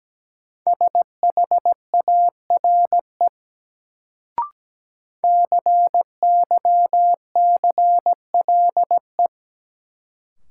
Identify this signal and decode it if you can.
Morse code, heard as a keyed tone, about 17 wpm, 700 Hz. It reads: SHARE CYCLE
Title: Sentences from Top 900 Words - New Word Drill - 17wpm